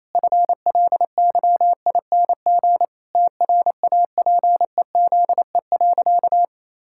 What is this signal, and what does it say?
Morse code, heard as a keyed tone, about 27 wpm, 700 Hz. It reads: FLYING TRAPEZE.
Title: Pangram Challenge v2 - 28wpm